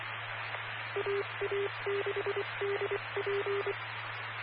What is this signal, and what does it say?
Morse code, heard as a keyed tone, about 24 wpm, 400 Hz. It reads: AA6DP
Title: Claremont, CA Police, Amateur Radio